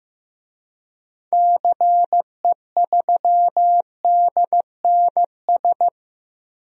Morse code, heard as a keyed tone, about 16 wpm, 700 Hz. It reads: CE3DNS
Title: DXCC-100 Call Signs - 6 Characters - Repeated 2x - 15wpm